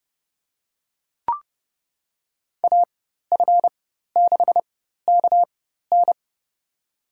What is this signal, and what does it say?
Morse code, heard as a keyed tone, about 30 wpm, spaced as for 14 wpm, 700 Hz. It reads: AF6KN
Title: USA Call Signs - 5 Characters - 4x Character Spacing - 30wpm